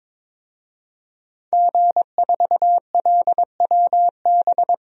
Morse code, heard as a keyed tone, about 22 wpm, 700 Hz. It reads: G4LWB
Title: DXCC-100 Call Signs - 5 Characters - 22wpm